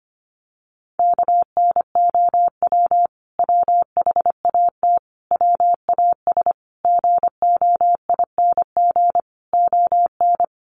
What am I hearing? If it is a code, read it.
Morse code, heard as a keyed tone, about 25 wpm, 700 Hz. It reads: KNOW WHAT WAS GOING ON
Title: Sets of 5 Words Encore2 25wpm